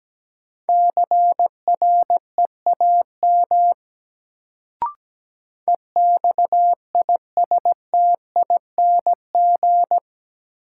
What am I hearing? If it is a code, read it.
Morse code, heard as a keyed tone, about 17 wpm, 700 Hz. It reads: CREAM EXISTING
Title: Sentences from Top 1500 Words - New Word Drill Alt - 17wpm